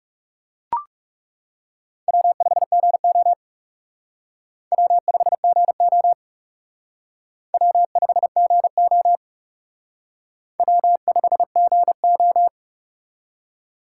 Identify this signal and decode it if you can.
Morse code, heard as a keyed tone, about 37 wpm, 700 Hz. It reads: W5GO W5GO W5GO W5GO
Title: Speed Racing - USA Call Signs 30wpm